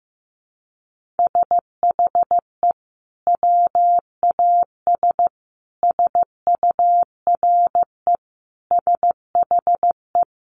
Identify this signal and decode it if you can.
Morse code, heard as a keyed tone, about 15 wpm, 700 Hz. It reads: SHE WAS SURE SHE
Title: Sentences from Top 1500 Words 15wpm